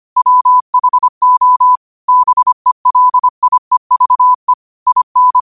Morse code, heard as a keyed tone, about 25 wpm, 1000 Hz. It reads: WHO BELIEVE IN
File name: qotdfiles / 25wpm